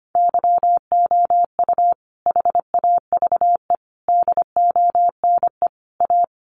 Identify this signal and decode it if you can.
Morse code, heard as a keyed tone, about 25 wpm, 700 Hz. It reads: YOU HAVE DONE A
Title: Sentences from Top 700 Words 25wpm